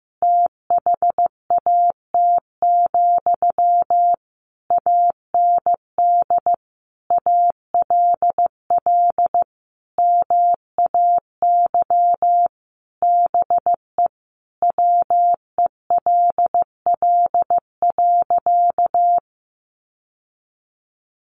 Morse code, heard as a keyed tone, about 15 wpm, 700 Hz. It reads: THAT, AND ALL MAY BE WELL.
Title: Sentences from Top 1200 Words 15wpm